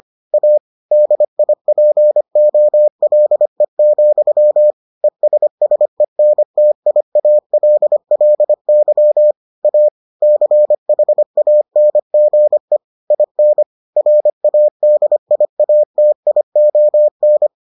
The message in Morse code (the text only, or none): A DIPOLE, ESSENTIALLY A CHANGE IN RADIATION